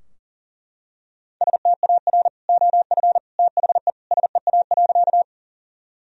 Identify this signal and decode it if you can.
Morse code, heard as a keyed tone, about 40 wpm, 700 Hz. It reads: STAR OF THE SEA.
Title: Sentences from Top 1000 Words - Review Part 2 - 40wpm